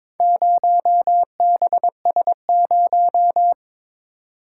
Morse code, heard as a keyed tone, about 22 wpm, 700 Hz. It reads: 0BS0
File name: practice / 22wpm / Four Letter-Number Combinations - 22wpm